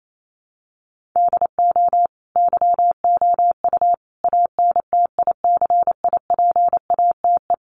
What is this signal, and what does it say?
Morse code, heard as a keyed tone, about 28 wpm, 700 Hz. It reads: DO YOU ANTICIPATE
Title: Sentences from Top 1500 Words 28wpm